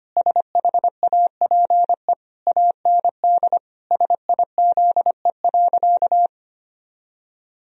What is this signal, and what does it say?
Morse code, heard as a keyed tone, about 25 wpm, 700 Hz. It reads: SHAPE AND SIZE.
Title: Sentences from Top 1000 Words - Review Part 3 - 25wpm